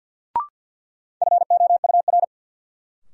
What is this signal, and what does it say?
Morse code, heard as a keyed tone, about 51 wpm, 700 Hz. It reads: FOUR
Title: Sentences from Top 300 Words - New Word Drill - 50wpm